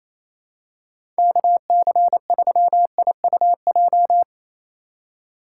Morse code, heard as a keyed tone, about 28 wpm, 700 Hz. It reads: KC3IUJ
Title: USA Call Signs - 6 Characters - 4x Character Spacing - 28wpm